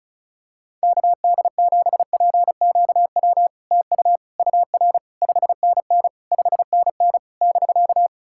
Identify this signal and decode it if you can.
Morse code, heard as a keyed tone, about 36 wpm, 690 Hz. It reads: KD7PQW TU UR 5NN 5NN <BK>
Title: SOTA QSO Practice 35wpm